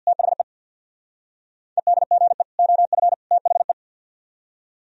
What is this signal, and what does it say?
Morse code, heard as a keyed tone, about 62 wpm, 700 Hz. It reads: THE EDGE OF THE
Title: Speed-Racing - Sets of 3 Words 50wpm